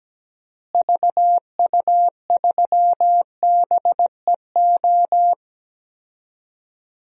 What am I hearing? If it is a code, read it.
Morse code, heard as a keyed tone, about 17 wpm, 700 Hz. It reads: VU3BEO
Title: DXCC-100 Call Signs - 6 Characters - Repeated 2x - 17wpm